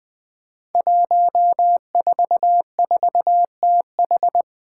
Morse code, heard as a keyed tone, about 20 wpm, 700 Hz. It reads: 144TH
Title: Five Challenging Letter-Number Combinations 20wpm